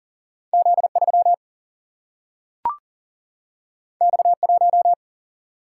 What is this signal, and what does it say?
Morse code, heard as a keyed tone, about 40 wpm, 700 Hz. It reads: Z3 X1
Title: Single-Letter Single-Number 40wpm